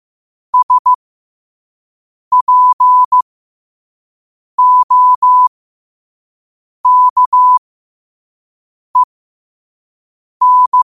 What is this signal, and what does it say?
Morse code, heard as a keyed tone, about 15 wpm, spaced as for 5 wpm, 1000 Hz. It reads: SPOKEN